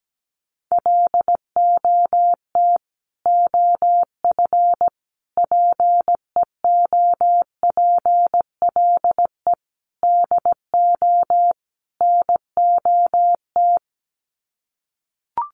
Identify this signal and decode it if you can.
Morse code, heard as a keyed tone, about 17 wpm, 700 Hz. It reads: LOT OF PEOPLE DO NOT
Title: Sets of 5 Words Encore 17wpm